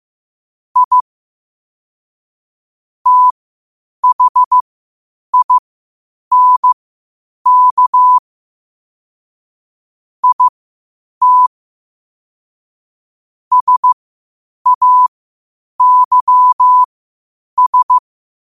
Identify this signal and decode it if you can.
Morse code, heard as a keyed tone, about 15 wpm, spaced as for 8 wpm, 1000 Hz. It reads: I THINK IT SAYS
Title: 8 WPM morse code quotes for Fri, 01 Mar 2024